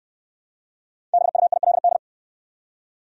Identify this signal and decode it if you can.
Morse code, heard as a keyed tone, about 69 wpm, 700 Hz. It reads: BUILD